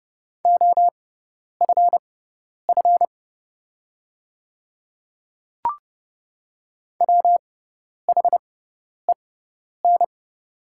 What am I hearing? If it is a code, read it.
Morse code, heard as a keyed tone, about 30 wpm, spaced as for 10 wpm, 700 Hz. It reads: OFF WHEN
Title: 6x Character Spacing - Top 300 Words - 30wpm